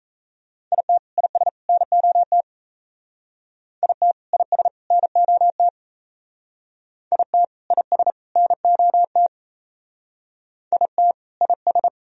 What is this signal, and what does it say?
Morse code, heard as a keyed tone, about 35 wpm, 700 Hz. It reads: IT IS NOT IT IS NOT IT IS NOT IT IS
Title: Speed-Racing - Sets of 3 Words 28wpm